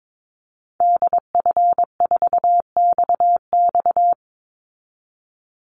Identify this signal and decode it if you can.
Morse code, heard as a keyed tone, about 22 wpm, 700 Hz. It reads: DF4XX
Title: Call Signs Repeated 3x - 22wpm